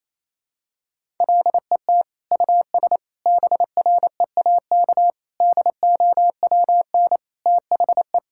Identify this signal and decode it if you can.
Morse code, heard as a keyed tone, about 27 wpm, 700 Hz. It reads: LET US BREAK DOWN THE
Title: Sentences from Top 1000 Words - Review Part 6 - 28wpm